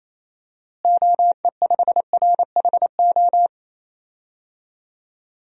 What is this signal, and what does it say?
Morse code, heard as a keyed tone, about 28 wpm, 700 Hz. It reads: OE5RHO
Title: DXCC-100 Call Signs - 6 Characters - Repeated 3x - 28wpm